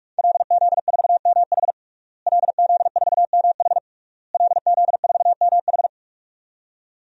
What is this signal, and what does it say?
Morse code, heard as a keyed tone, about 42 wpm, 700 Hz. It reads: LZ4MH LZ4MH LZ4MH
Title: DXCC-100 Call Signs - 5 Characters - Repeated 3x - 45wpm